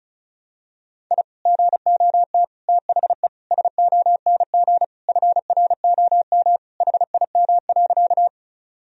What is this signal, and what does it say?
Morse code, heard as a keyed tone, about 34 wpm, 700 Hz. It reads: I GOT THE SONG FROM HIM.
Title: Sentences from Top 1000 Words - Review Part 2 - 35wpm